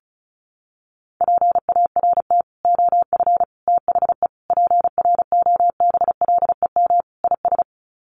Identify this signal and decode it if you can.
Morse code, heard as a keyed tone, about 34 wpm, 700 Hz. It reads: PART OF THE PROBLEM IS
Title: Sets of 5 Words Encore2 35wpm